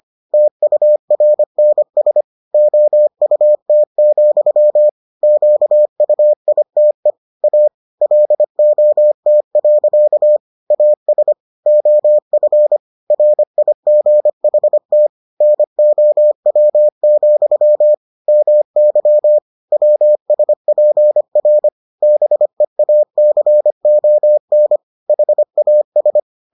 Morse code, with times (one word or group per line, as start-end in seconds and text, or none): 0.3-2.2 TURNS
2.5-4.9 OUT,
5.2-7.1 QUITE
7.4-7.7 A
8.0-10.4 LOT.
10.7-11.3 AS
11.7-12.8 OF
13.1-15.1 RIGHT
15.4-17.9 NOW,
18.3-19.4 MY
19.7-21.7 WSPR
22.0-24.8 BEACON
25.1-26.2 HAS